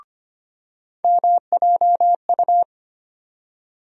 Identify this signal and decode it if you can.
Morse code, heard as a keyed tone, about 25 wpm, 700 Hz. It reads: MJU
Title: USA Call Sign Postfixes - 3 Letters 25wpm